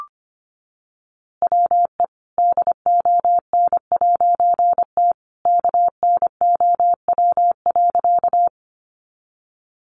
Morse code, heard as a keyed tone, about 25 wpm, 700 Hz. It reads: WE DON'T KNOW.
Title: A Sound Of Thunder 25wpm